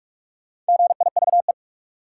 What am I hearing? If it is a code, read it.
Morse code, heard as a keyed tone, about 45 wpm, 700 Hz. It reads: GIVE
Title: Sets of 2 Words - New Word Drill - 45wpm